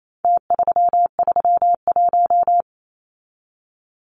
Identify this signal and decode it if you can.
Morse code, heard as a keyed tone, about 28 wpm, 700 Hz. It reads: T331
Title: Numbers - Contest Style - 4-Digit 28wpm